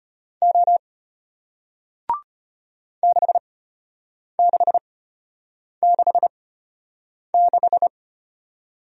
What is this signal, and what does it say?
Morse code, heard as a keyed tone, about 33 wpm, spaced as for 8 wpm, 700 Hz. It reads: O 6666